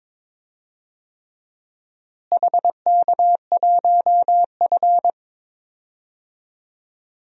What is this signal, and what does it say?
Morse code, heard as a keyed tone, about 22 wpm, 700 Hz. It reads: HK1F
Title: DXCC-100 Call Signs - 4 Characters - Repeated 3x - 22wpm